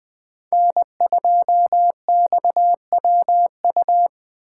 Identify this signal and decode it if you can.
Morse code, heard as a keyed tone, about 20 wpm, 700 Hz. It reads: N2XWU